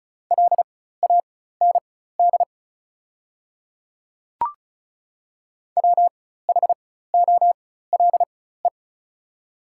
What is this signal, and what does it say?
Morse code, heard as a keyed tone, about 34 wpm, spaced as for 16 wpm, 690 Hz. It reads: LAND WHOLE
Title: 4x Character Spacing - Top 500 Words - 35wpm